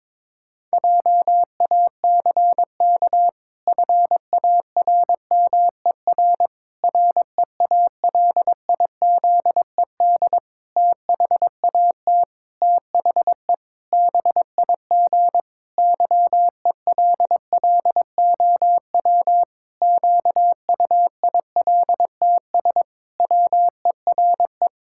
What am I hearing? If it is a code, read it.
Morse code, heard as a keyed tone, about 22 wpm, 700 Hz. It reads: JACK FARMER REALIZED THAT THE BIG YELLOW QUILTS WERE
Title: Pangram Challenge v2 - 22wpm